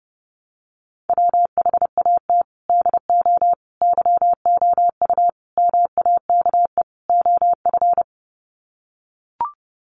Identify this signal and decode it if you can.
Morse code, heard as a keyed tone, about 30 wpm, 700 Hz. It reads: WHAT DO YOU MAKE OF
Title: Sets of 5 Words 30wpm